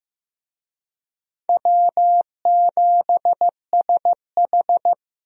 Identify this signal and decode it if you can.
Morse code, heard as a keyed tone, about 15 wpm, 700 Hz. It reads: W7SH
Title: USA Call Signs - 4 Characters - Repeated 3x - 15wpm